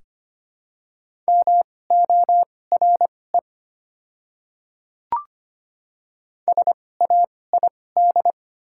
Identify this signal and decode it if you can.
Morse code, heard as a keyed tone, about 25 wpm, spaced as for 18 wpm, 690 Hz. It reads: MORE SAID
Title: 2x Character Spacing - Top 200 Words - 25wpm